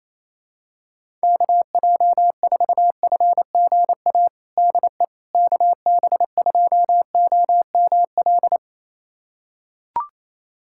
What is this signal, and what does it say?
Morse code, heard as a keyed tone, about 27 wpm, 700 Hz. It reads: KJ4FGA DE KB2OML
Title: USA Callsign DE Callsign - 6 Characters - 28wpm